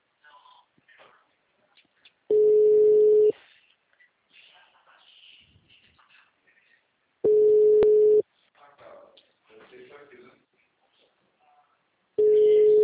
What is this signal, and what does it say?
Ringback tone (Europe, 425 Hz), with speech.
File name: out- -2015- - - .28